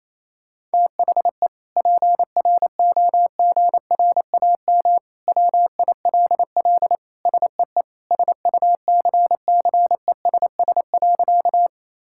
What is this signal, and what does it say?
Morse code, heard as a keyed tone, about 28 wpm, 700 Hz. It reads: THE PROGRAM WILL SEE SUCCESS.
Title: Sentences from Top 1000 Words - Review Part 5 - 28wpm